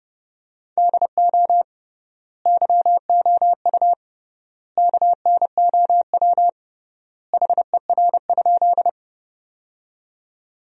Morse code, heard as a keyed tone, about 32 wpm, 700 Hz. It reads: DO YOU KNOW HER?
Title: 3x Word-Spacing - Sentences from Top 100 Words - 30wpm